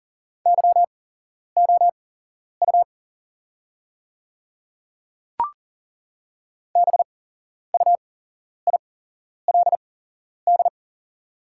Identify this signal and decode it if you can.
Morse code, heard as a keyed tone, about 40 wpm, spaced as for 11 wpm, 690 Hz. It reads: YOU BUILD
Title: 8x Character Spacing - Top 200 Words - 40wpm